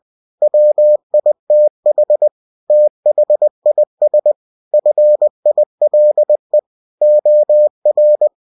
Morse code, heard as a keyed tone, about 20 wpm, 600 Hz. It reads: WITH THIS FILE OR
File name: cw-books / winnie-the-pooh / winnie-20wpm-page-0113